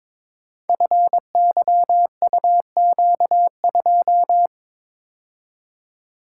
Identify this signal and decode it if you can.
Morse code, heard as a keyed tone, about 22 wpm, 700 Hz. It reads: FYUQ2
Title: Code Groups - 5 Letter:Numbers - 22wpm